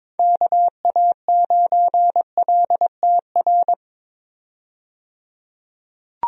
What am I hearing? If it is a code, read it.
Morse code, heard as a keyed tone, about 22 wpm, 700 Hz. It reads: KA9LTR
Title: USA Call Signs - 6 Characters - 22wpm